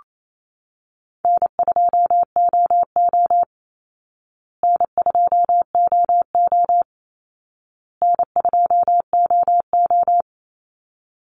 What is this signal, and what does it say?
Morse code, heard as a keyed tone, about 28 wpm, 700 Hz. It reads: N2OO N2OO N2OO
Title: Easy Call Signs Repeated 3x - 28wpm